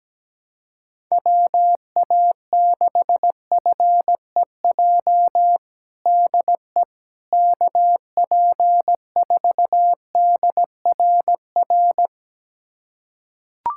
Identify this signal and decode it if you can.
Morse code, heard as a keyed tone, about 17 wpm, 700 Hz. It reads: WA6FEJ DE KP4DRR
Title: USA Callsign DE Callsign - 6 Characters - 17wpm